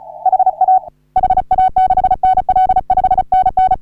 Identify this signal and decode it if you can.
Morse code, heard as a keyed tone, about 37 wpm, 750 Hz. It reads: HA HA6NL 5NN
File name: CW / HA3OU RHOBCW REC